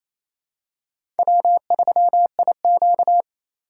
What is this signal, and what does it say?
Morse code, heard as a keyed tone, about 28 wpm, 700 Hz. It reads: W3IQ
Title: USA Call Signs - 4 Characters - 4x Character Spacing - 28wpm